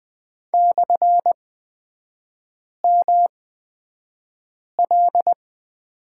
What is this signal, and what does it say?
Morse code, heard as a keyed tone, about 20 wpm, spaced as for 5 wpm, 700 Hz. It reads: /ML
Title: 255 Lesson 41e - Period - Three Characters 20wpm